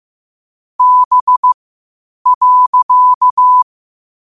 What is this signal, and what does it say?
Morse code, heard as a keyed tone, about 15 wpm, spaced as for 8 wpm, 1000 Hz. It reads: B.
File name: qotdfiles / 8wpm